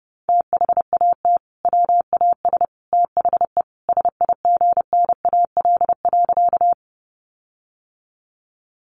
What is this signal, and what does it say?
Morse code, heard as a keyed tone, about 30 wpm, 700 Hz. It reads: THAT WAS THE SIGNAL.
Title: Sentences from Top 1400 Words 30wpm